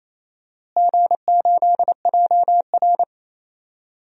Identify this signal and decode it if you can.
Morse code, heard as a keyed tone, about 28 wpm, 700 Hz. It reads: G8JR
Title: DXCC-100 Call Signs - 4 Characters - Repeated 2x - 28wpm